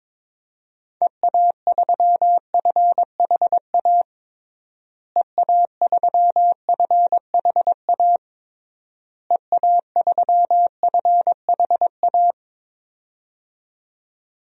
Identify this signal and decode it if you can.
Morse code, heard as a keyed tone, about 22 wpm, 700 Hz. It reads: EA3FHA EA3FHA EA3FHA
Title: DXCC-100 Call Signs - 6 Characters - Repeated 3x - 22wpm